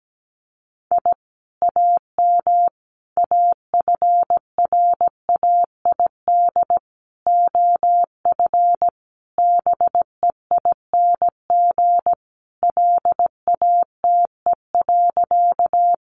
Morse code, heard as a keyed tone, about 17 wpm, 700 Hz. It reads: I AM AFRAID OF BEING LATE.
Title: Sentences from Top 1500 Words 17wpm